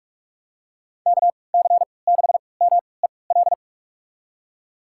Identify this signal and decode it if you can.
Morse code, heard as a keyed tone, about 46 wpm, spaced as for 25 wpm, 700 Hz. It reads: KC6MEL
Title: USA Call Signs - 6 Characters - 3x Character Spacing - 45wpm